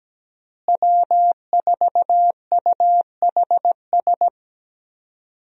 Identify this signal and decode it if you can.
Morse code, heard as a keyed tone, about 17 wpm, 700 Hz. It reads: W4UHS
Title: USA Call Signs - 5 Characters - Repeated 2x - 17wpm